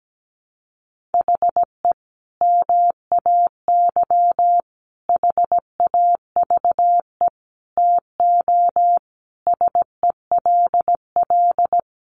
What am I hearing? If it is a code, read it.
Morse code, heard as a keyed tone, about 17 wpm, 700 Hz. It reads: HE MAY HAVE TO SELL
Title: Sentences from Top 1100 Words 17wpm